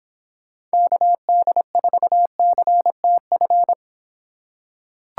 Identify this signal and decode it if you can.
Morse code, heard as a keyed tone, about 25 wpm, 700 Hz. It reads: KD4CTF